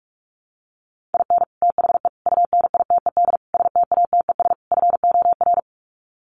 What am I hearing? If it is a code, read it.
Morse code, heard as a keyed tone, about 44 wpm, 700 Hz. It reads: IN THE UNITED STATES FOR
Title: Sets of 5 Words Encore2 45wpm